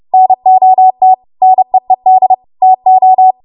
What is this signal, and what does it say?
Morse code, heard as a keyed tone, about 30 wpm, 750 Hz. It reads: NOT NEED TO
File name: Morse / 30WPM